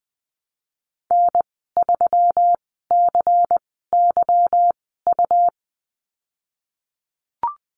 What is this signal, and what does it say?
Morse code, heard as a keyed tone, about 19 wpm, spaced as for 14 wpm, 700 Hz. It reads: N3CYU
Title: 147 Lesson 24g - Letter y - Call Signs 20wpm